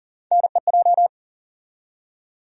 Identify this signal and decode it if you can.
Morse code, heard as a keyed tone, about 40 wpm, 700 Hz. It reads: NEJ
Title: USA Call Sign Postfixes - 3 Letters 40wpm